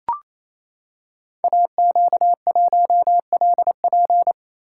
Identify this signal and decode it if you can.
Morse code, heard as a keyed tone, about 29 wpm, 700 Hz. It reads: AQ1LP
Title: Code Groups - 5 Letter:Numbers - 28wpm